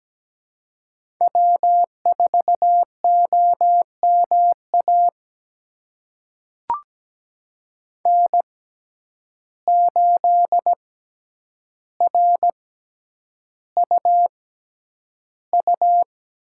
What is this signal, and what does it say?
Morse code, heard as a keyed tone, about 17 wpm, spaced as for 9 wpm, 700 Hz. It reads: W4OMA N8RUU
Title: USA Call Signs - 5 Characters - 6x Character Spacing - 17wpm